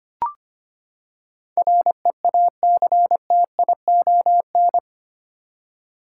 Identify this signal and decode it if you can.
Morse code, heard as a keyed tone, about 25 wpm, 700 Hz. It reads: REACTION
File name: Eight Letter Words - 25wpm